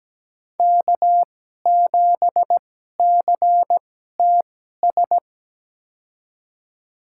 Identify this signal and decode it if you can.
Morse code, heard as a keyed tone, about 17 wpm, spaced as for 12 wpm, 700 Hz. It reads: K7CTS
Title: USA Call Signs - 5 Characters - 2x Character Spacing - 17wpm